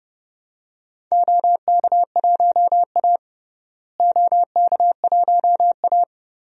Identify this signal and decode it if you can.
Morse code, heard as a keyed tone, about 30 wpm, 700 Hz. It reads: OK1A OK1A
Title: DXCC-100 Call Signs - 4 Characters - Repeated 2x - 30wpm